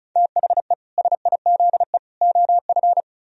Morse code, heard as a keyed tone, about 46 wpm, 700 Hz. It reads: THE SIZE OF